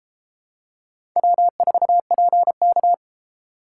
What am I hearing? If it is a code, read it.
Morse code, heard as a keyed tone, about 32 wpm, 700 Hz. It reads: W4PK